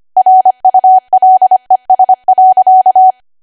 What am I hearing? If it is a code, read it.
Morse code, heard as a keyed tone, about 25 wpm, 750 Hz. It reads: RULES.